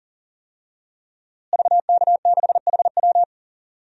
Morse code, heard as a keyed tone, about 40 wpm, 700 Hz. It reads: VK6HW
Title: DXCC-100 Call Signs - 5 Characters - Repeated 2x - 40wpm